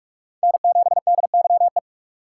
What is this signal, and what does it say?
Morse code, heard as a keyed tone, about 44 wpm, 700 Hz. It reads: N7DYE